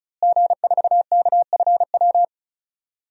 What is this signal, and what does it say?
Morse code, heard as a keyed tone, about 34 wpm, 700 Hz. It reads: G4KFW
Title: DXCC-100 Call Signs - 5 Characters - Repeated 2x - 35wpm